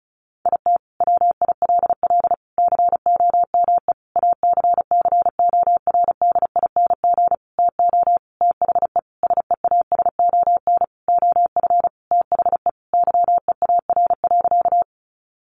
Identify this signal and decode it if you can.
Morse code, heard as a keyed tone, about 34 wpm, 700 Hz. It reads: IT WILL COME ACCORDING TO THE SEASON OF THE YEAR.
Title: Sentences from Top 1400 Words 35wpm